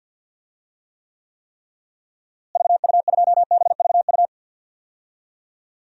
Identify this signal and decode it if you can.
Morse code, heard as a keyed tone, about 46 wpm, 700 Hz. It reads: VU2BVU
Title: DXCC-100 Call Signs - 6 Characters - Repeated 3x - 50wpm